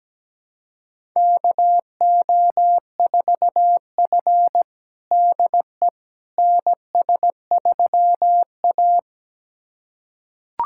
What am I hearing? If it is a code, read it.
Morse code, heard as a keyed tone, about 17 wpm, 700 Hz. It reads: KO4F DE NS3A
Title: USA Callsign DE Callsign - 4 Characters - 17wpm